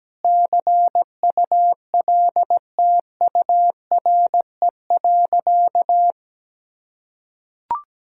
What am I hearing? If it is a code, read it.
Morse code, heard as a keyed tone, about 17 wpm, 700 Hz. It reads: CULTURE.
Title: Sentences from Top 1000 Words - Review Part 6 - 17wpm